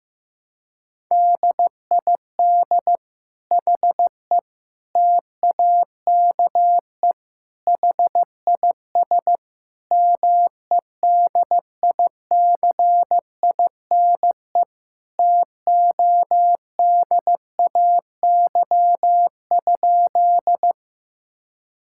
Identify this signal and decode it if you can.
Morse code, heard as a keyed tone, about 15 wpm, 700 Hz. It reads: DID HE TAKE HIS MEDICINE TODAY?
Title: Sentences from Top 1000 Words - Review Part 6 - 15wpm